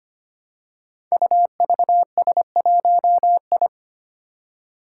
Morse code, heard as a keyed tone, about 25 wpm, 700 Hz. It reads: UVS1I
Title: Five Challenging Letter-Number Combinations 25wpm